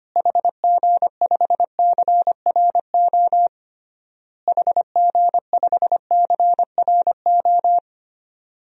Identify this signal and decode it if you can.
Morse code, heard as a keyed tone, about 25 wpm, 700 Hz. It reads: HG5CRO HG5CRO
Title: DXCC-100 Call Signs - 6 Characters - Repeated 3x - 25wpm